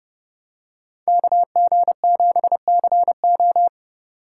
Morse code, heard as a keyed tone, about 29 wpm, 700 Hz. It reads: KG7CO